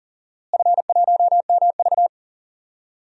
Morse code, heard as a keyed tone, about 40 wpm, 700 Hz. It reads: F1MV
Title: DXCC-100 Call Signs - 4 Characters - Repeated 2x - 40wpm